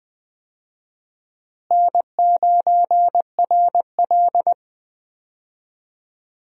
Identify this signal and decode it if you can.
Morse code, heard as a keyed tone, about 20 wpm, 700 Hz. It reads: N9RL
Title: USA Call Signs - 4 Characters - Repeated 2x - 20wpm